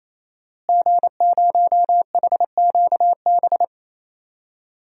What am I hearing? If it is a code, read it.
Morse code, heard as a keyed tone, about 27 wpm, 700 Hz. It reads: G0HQB
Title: DXCC-100 Call Signs - 5 Characters - Repeated 3x - 28wpm